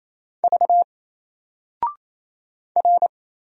Morse code, heard as a keyed tone, about 28 wpm, 700 Hz. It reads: V R
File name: Single-Letter-Number-28wpm